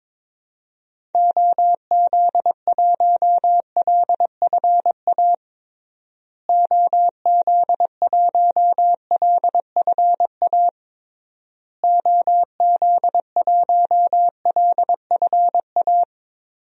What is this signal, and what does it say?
Morse code, heard as a keyed tone, about 22 wpm, 700 Hz. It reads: OZ1LFA OZ1LFA OZ1LFA
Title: DXCC-100 Call Signs - 6 Characters - Repeated 3x - 22wpm